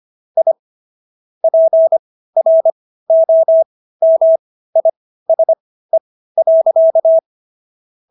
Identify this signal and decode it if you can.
Morse code, heard as a keyed tone, about 25 wpm, spaced as for 15 wpm, 650 Hz. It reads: I PROMISE.